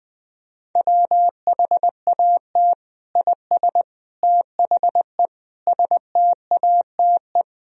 Morse code, heard as a keyed tone, about 20 wpm, 700 Hz. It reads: WHAT IS THE STATE
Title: Sentences from Top 1000 Words - Review Part 3 - 20wpm